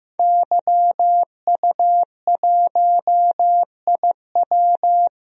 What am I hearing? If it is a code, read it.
Morse code, heard as a keyed tone, about 15 wpm, 700 Hz. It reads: YU1IW
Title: DXCC-100 Call Signs - 5 Characters - 15wpm